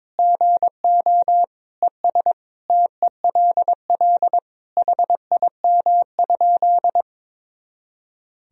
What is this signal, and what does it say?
Morse code, heard as a keyed tone, about 22 wpm, 700 Hz. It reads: GO ES TELL HIM?
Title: Sentences from Top 200 Words and Abbreviations 22wpm